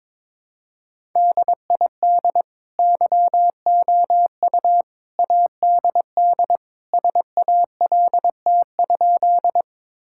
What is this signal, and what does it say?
Morse code, heard as a keyed tone, about 22 wpm, 700 Hz. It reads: DID YOU ADD SALT?
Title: Sentences from Top 1000 Words - Review Part 5 - 22wpm